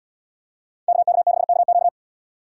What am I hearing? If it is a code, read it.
Morse code, heard as a keyed tone, about 72 wpm, 700 Hz. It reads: KK6KQ